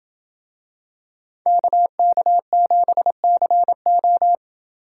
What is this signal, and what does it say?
Morse code, heard as a keyed tone, about 26 wpm, 700 Hz. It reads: KK7CO